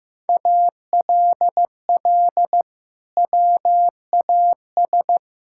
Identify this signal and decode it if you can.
Morse code, heard as a keyed tone, about 15 wpm, 700 Hz. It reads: ALL WAS
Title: Sentences from Top 1000 Words - Review Part 6 - 15wpm